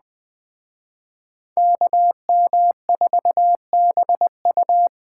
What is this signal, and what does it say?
Morse code, heard as a keyed tone, about 20 wpm, 700 Hz. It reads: KM4BU
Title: Speed Racing - USA 2x2 Call Signs 17wpm